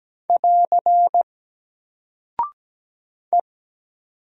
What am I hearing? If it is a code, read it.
Morse code, heard as a keyed tone, about 17 wpm, 700 Hz. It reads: <AR> E
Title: Single Character (Letter:Number:Punctuation:Prosign) 17wpm